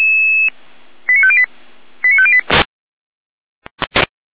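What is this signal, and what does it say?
Five-tone selective call (ZVEI), heard as quick runs of five tones, tones of about 70 ms. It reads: 89508 89508
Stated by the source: ZVEI five-tone call 89153 received by SDR.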